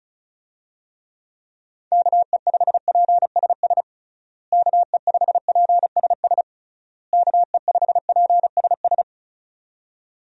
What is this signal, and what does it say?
Morse code, heard as a keyed tone, about 34 wpm, 700 Hz. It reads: KE5PSS KE5PSS KE5PSS
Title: USA Call Signs - 6 Characters - Repeated 3x - 35wpm